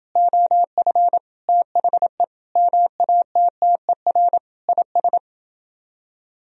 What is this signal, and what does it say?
Morse code, heard as a keyed tone, about 31 wpm, 700 Hz. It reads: OF THE MATTER IS